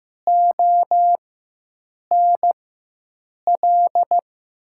Morse code, heard as a keyed tone, about 15 wpm, spaced as for 7 wpm, 700 Hz. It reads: ONL
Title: USA Call Signs - 6 Characters - 4x Character Spacing - 15wpm